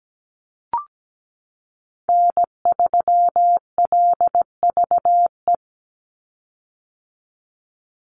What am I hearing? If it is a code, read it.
Morse code, heard as a keyed tone, about 17 wpm, 700 Hz. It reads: N3LVE
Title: USA Call Signs 17wpm